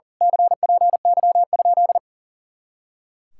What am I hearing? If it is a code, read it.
Morse code, heard as a keyed tone, about 35 wpm, 700 Hz. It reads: CPY?
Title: Speed-Racing Common QSO Phrases Encore - 40wpm